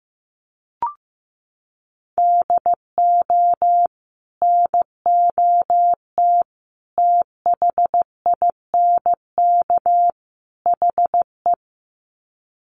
Morse code, heard as a keyed tone, about 15 wpm, 700 Hz. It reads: DO NOT THINK HE
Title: Sets of 4 Words 15wpm